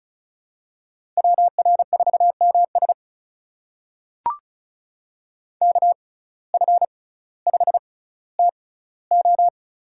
Morse code, heard as a keyed tone, about 36 wpm, spaced as for 18 wpm, 700 Hz. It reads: WR4MS KF5TO
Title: USA Call Signs - 5 Characters - 6x Character Spacing - 35wpm